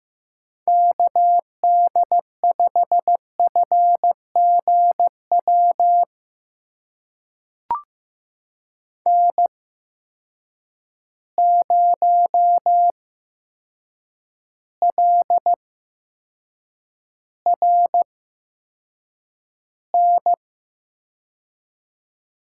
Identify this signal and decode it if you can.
Morse code, heard as a keyed tone, about 15 wpm, spaced as for 7 wpm, 700 Hz. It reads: KD5FGW N0LRN
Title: USA Call Signs - 8x Character Spacing - 15wpm